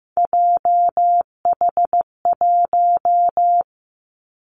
Morse code, heard as a keyed tone, about 15 wpm, 700 Hz. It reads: JH1
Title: Callsign Prefixes for Top 25 DXCC - 15wpm